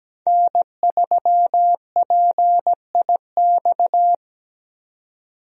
Morse code, heard as a keyed tone, about 17 wpm, 700 Hz. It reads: N3PIX
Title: USA Call Signs - 5 Characters - 17wpm